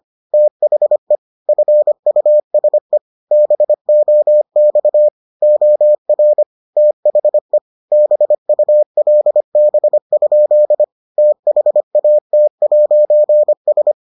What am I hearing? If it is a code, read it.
Morse code, heard as a keyed tone, about 25 wpm, 600 Hz. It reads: THE FUSE BOX OR THE BULB? THAT'S